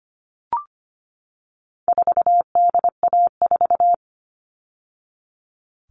Morse code, heard as a keyed tone, about 25 wpm, 700 Hz. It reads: 4DA4
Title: All Cut Numbers - 4-Digit 25wpm